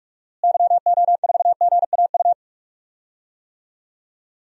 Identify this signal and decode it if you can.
Morse code, heard as a keyed tone, about 46 wpm, 700 Hz. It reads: YO4GAU
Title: DXCC-100 Call Signs - 6 Characters - Repeated 3x - 45wpm